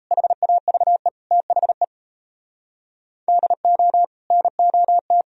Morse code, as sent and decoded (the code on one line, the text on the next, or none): .... .- ...- . / - .... / -.. --- / -. --- -
HAVE THE DO NOT